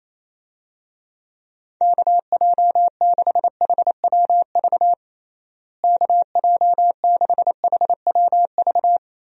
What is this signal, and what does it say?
Morse code, heard as a keyed tone, about 28 wpm, 700 Hz. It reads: KJ6HWV KJ6HWV
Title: USA Call Signs - 6 Characters - Repeated 2x - 28wpm